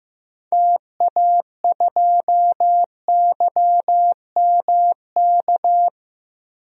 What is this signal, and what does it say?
Morse code, heard as a keyed tone, about 15 wpm, 700 Hz. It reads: TA2YMK
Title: DXCC-100 Call Signs - 6 Characters - 15wpm